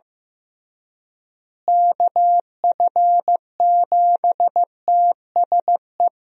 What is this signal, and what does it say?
Morse code, heard as a keyed tone, about 15 wpm, 700 Hz. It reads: KF7TSE
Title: USA Call Signs - 6 Characters - Repeated 3x - 15wpm